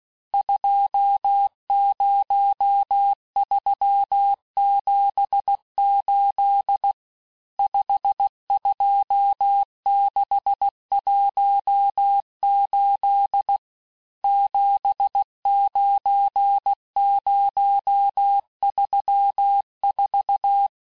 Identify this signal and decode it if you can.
Morse code, heard as a keyed tone, about 16 wpm, 800 Hz. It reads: 20378 52618 79034
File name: cours-du-lundi files / Cours-1586-texte-chiffre-vitesse-15mpm